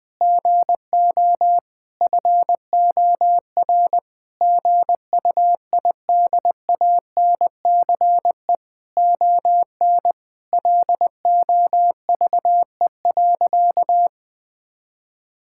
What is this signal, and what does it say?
Morse code, heard as a keyed tone, about 20 wpm, 700 Hz. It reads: GO FOR GUIDANCE ON LOVE.
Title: Sentences from Top 1600 Words 20wpm